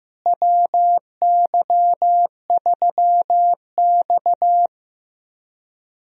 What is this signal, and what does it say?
Morse code, heard as a keyed tone, about 15 wpm, 700 Hz. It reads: WY3X
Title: USA Call Signs - 4 Characters - Repeated 2x - 15wpm